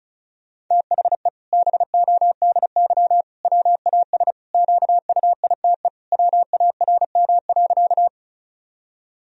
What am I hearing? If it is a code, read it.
Morse code, heard as a keyed tone, about 34 wpm, 700 Hz. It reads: THE BODY WAS QUITE WARM.
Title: Sentences from Top 1000 Words - Review Part 2 - 35wpm